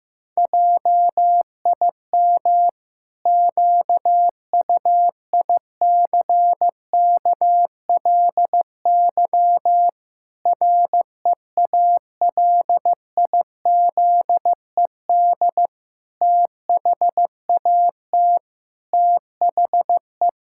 Morse code, heard as a keyed tone, about 16 wpm, 700 Hz. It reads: JIM QUICKLY REALIZED THAT THE
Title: Pangram Challenge v2 - 15wpm